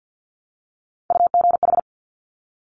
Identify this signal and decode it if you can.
Morse code, heard as a keyed tone, about 50 wpm, 700 Hz. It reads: UGH
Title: Top Three Letter Combinations in Words 50wpm